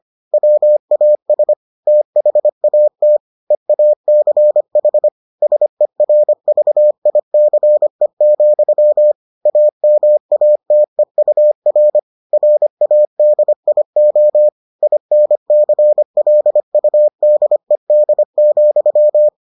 Morse code, heard as a keyed tone, about 25 wpm, 600 Hz. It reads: WAS THAT EACH SERVICE, AMATEUR RADIO INCLUDED,